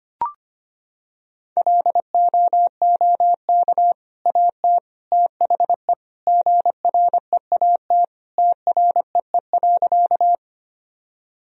Morse code, heard as a keyed tone, about 25 wpm, 700 Hz. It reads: LOOK AT THE GREAT TREE.
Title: Sentences from Top 1000 Words - Review Part 1 - 25wpm